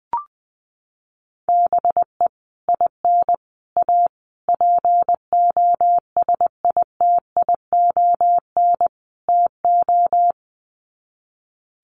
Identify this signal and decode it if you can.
Morse code, heard as a keyed tone, about 20 wpm, 700 Hz. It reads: BE IN A POSITION TO
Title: Sets of 5 Words Encore2 20wpm